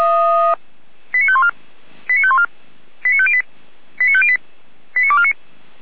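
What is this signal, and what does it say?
Five-tone selective call (ZVEI), heard as quick runs of five tones, tones of about 70 ms. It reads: 89514 89514 89508 89508 89249
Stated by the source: ZVEI five-tone selective call received on an SDR.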